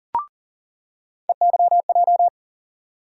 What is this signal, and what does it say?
Morse code, heard as a keyed tone, about 40 wpm, 700 Hz. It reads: EYJ
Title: USA Call Sign Postfixes - 3 Letters 40wpm